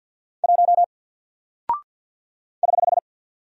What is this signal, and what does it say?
Morse code, heard as a keyed tone, about 48 wpm, 700 Hz. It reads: 1 <HH>
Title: Single Character (Letter:Number:Punctuation:Prosign) Alt 50wpm